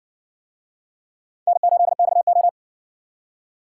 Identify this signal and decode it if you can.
Morse code, heard as a keyed tone, about 60 wpm, 700 Hz. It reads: N8XO